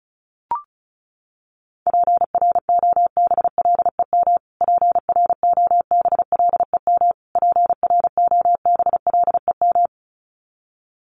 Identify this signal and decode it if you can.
Morse code, heard as a keyed tone, about 34 wpm, 700 Hz. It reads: PROBLEM PROBLEM PROBLEM
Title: Top 500 Words Repeated 35wpm